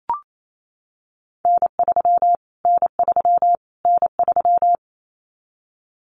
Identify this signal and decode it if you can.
Morse code, heard as a keyed tone, about 28 wpm, 700 Hz. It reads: N3 N3 N3
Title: Callsign Prefixes for Top 25 DXCC - Repeated 3x - 28wpm